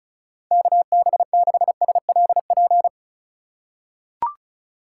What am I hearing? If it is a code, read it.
Morse code, heard as a keyed tone, about 36 wpm, 700 Hz. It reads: KB6SLP
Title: USA Call Signs - 6 Characters - 3x Character Spacing - 35wpm